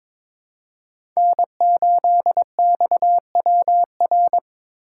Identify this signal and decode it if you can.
Morse code, heard as a keyed tone, about 22 wpm, 700 Hz. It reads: N8XWR
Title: USA Call Signs - 5 Characters - 4x Character Spacing - 22wpm